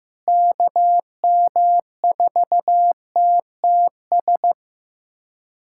Morse code, heard as a keyed tone, about 15 wpm, 700 Hz. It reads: KM4TTS
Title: USA Callsign DE Callsign - 6 Characters - 15wpm